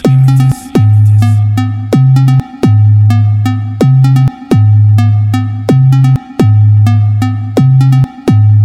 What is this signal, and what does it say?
There is no code.